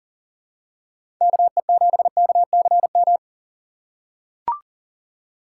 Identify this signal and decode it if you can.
Morse code, heard as a keyed tone, about 40 wpm, 700 Hz. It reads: KE7KCM
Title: USA Call Signs - 2x Character Spacing - 40wpm